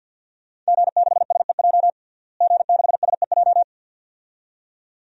Morse code, heard as a keyed tone, about 46 wpm, 700 Hz. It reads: G6SEJ G6SEJ
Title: DXCC-100 Call Signs - 5 Characters - Repeated 2x - 50wpm